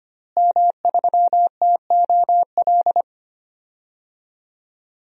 Morse code, heard as a keyed tone, about 25 wpm, 700 Hz. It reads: M3TOL
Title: DXCC-100 Call Signs - 5 Characters - Repeated 3x - 25wpm